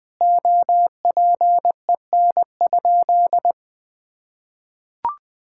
Morse code, heard as a keyed tone, about 20 wpm, 700 Hz. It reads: OPEN?
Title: Sentences with 2-Word Phrases - 20wpm